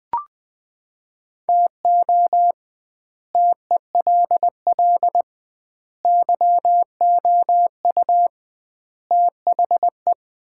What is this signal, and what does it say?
Morse code, heard as a keyed tone, about 20 wpm, 700 Hz. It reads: TO TELL YOU THE
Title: 2x Word-Spacing - Sets of 5 Words - 20wpm